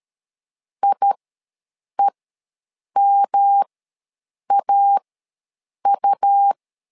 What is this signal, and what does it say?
Morse code, heard as a keyed tone, about 13 wpm, spaced as for 7 wpm, 800 Hz. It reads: IEMAU
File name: LEZIONI / 3 MWRO / 3 M ws 12wpm 1745